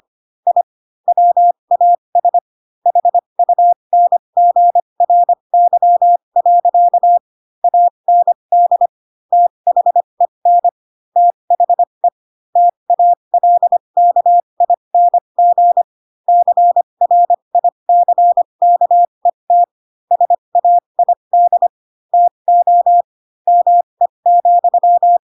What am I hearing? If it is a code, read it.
Morse code, heard as a keyed tone, about 25 wpm, spaced as for 22 wpm, 700 Hz. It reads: I WAS HUNGRY. AND THEN THE TALKING CRICKET SAID TO ME,
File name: Pinocchio / Pinocchio ch7 18 25 ik1hge